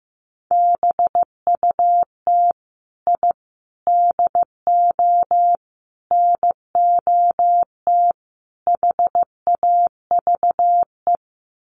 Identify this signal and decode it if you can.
Morse code, heard as a keyed tone, about 15 wpm, 700 Hz. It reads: BUT I DO NOT HAVE
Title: Sets of 5 Words Encore 15wpm